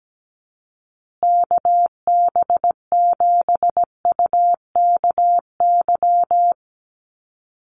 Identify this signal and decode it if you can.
Morse code, heard as a keyed tone, about 17 wpm, 700 Hz. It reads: KB7UKY
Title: USA Call Signs Repeated 2x 17wpm